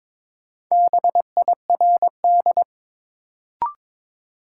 Morse code, heard as a keyed tone, about 22 wpm, 700 Hz. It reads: BIRD
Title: Sentences from Top 400 Words - New Word Drill - 22wpm